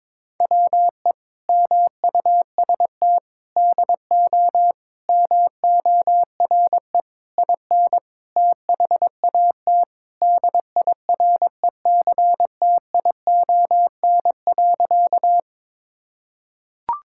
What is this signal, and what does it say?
Morse code, heard as a keyed tone, about 22 wpm, 700 Hz. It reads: WE MUST DO MORE IN THAT DIRECTION.
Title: Sentences from Top 1000 Words - Review Part 5 - 22wpm